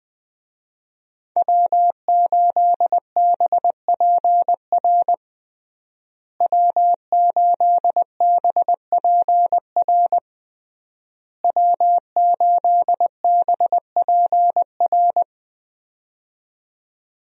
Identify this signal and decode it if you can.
Morse code, heard as a keyed tone, about 20 wpm, 700 Hz. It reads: W8BPR W8BPR W8BPR
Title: USA Call Signs - 5 Characters - Repeated 3x - 20wpm